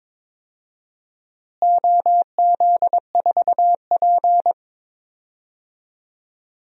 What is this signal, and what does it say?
Morse code, heard as a keyed tone, about 22 wpm, 700 Hz. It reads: OZ4P
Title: DXCC-100 Call Signs - 4 Characters - Repeated 2x - 22wpm